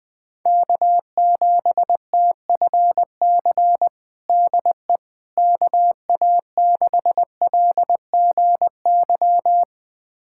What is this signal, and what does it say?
Morse code, heard as a keyed tone, about 21 wpm, 700 Hz. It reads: K7TFC DE KA6LGY
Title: USA Callsign DE Callsign 20wpm